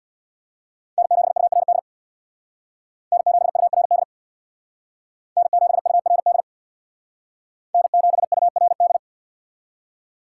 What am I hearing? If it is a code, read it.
Morse code, heard as a keyed tone, about 61 wpm, 700 Hz. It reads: N7URD N7URD N7URD N7URD
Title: Speed Racing - USA 1x3 Call Signs 50wpm